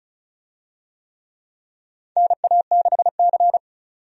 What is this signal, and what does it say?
Morse code, heard as a keyed tone, about 36 wpm, 700 Hz. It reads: NA6C
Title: USA Call Signs - 4 Characters - 35wpm